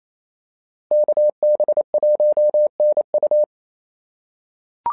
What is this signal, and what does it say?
Morse code, heard as a keyed tone, about 27 wpm, 590 Hz. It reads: KB1NU
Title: SOTA QSO Practice 28wpm